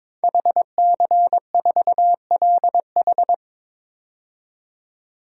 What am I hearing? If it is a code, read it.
Morse code, heard as a keyed tone, about 22 wpm, 700 Hz. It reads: HC4LH
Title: DXCC-100 Call Signs - 5 Characters - 22wpm